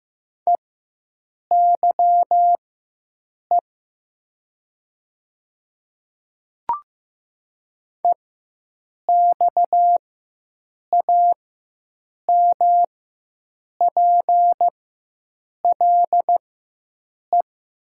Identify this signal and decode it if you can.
Morse code, heard as a keyed tone, about 15 wpm, spaced as for 7 wpm, 700 Hz. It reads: EYE EXAMPLE
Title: 4x Character Spacing - Top 300 Words - 15wpm